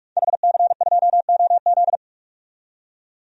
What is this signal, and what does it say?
Morse code, heard as a keyed tone, about 46 wpm, 700 Hz. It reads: HC2OZ
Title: DXCC-100 Call Signs - 5 Characters - 45wpm